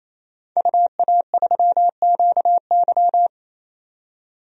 Morse code, heard as a keyed tone, about 28 wpm, 700 Hz. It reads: UA3QY
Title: DXCC-100 Call Signs - 5 Characters - 28wpm